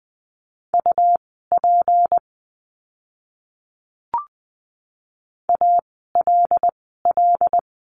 Morse code, heard as a keyed tone, about 20 wpm, spaced as for 15 wpm, 700 Hz. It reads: UP ALL
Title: 160 Lesson 26f - Letter g - Words 20wpm